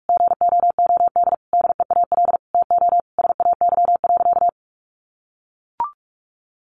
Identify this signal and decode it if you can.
Morse code, heard as a keyed tone, about 44 wpm, 700 Hz. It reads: GOOD DEAL TO SAY.
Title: Sentences from Top 1000 Words 45wpm